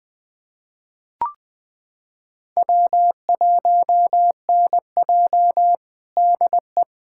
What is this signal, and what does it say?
Morse code, heard as a keyed tone, about 20 wpm, 700 Hz. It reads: W1NJ DE
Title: USA Callsign DE Callsign - 4 Characters - 20wpm